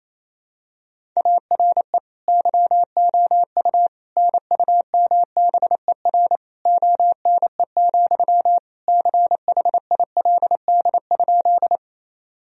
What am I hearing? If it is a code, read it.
Morse code, heard as a keyed tone, about 28 wpm, 700 Hz. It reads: ARE YOU NUMBER ONE, CHILD?
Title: Sentences from Top 1000 Words - Review Part 1 - 28wpm